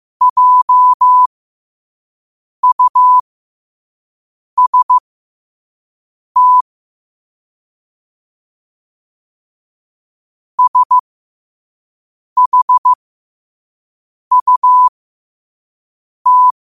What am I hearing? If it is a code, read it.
Morse code, heard as a keyed tone, about 15 wpm, spaced as for 5 wpm, 1000 Hz. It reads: JUST SHUT
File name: qotdfiles / 5wpm